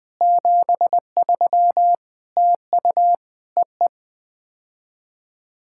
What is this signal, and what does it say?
Morse code, heard as a keyed tone, about 20 wpm, 700 Hz. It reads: 73 TU EE
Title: POTA QSO Practice - 20wpm